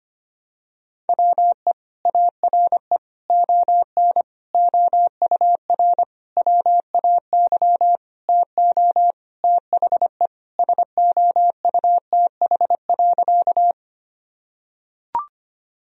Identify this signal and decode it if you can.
Morse code, heard as a keyed tone, about 25 wpm, 700 Hz. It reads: WE ARE ON OUR WAY TO THE SOUTH.
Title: Sentences from Top 1000 Words - Review Part 2 - 25wpm